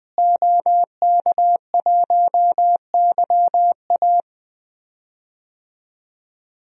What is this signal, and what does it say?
Morse code, heard as a keyed tone, about 21 wpm, 700 Hz. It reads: OK1YA
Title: DXCC-100 Call Signs - 5 Characters - Repeated 2x - 20wpm